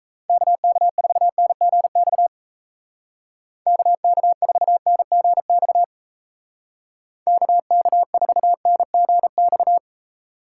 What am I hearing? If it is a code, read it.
Morse code, heard as a keyed tone, about 34 wpm, 700 Hz. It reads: KK4NGX KK4NGX KK4NGX
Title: Speed Racing - USA 2x3 Call Signs 28wpm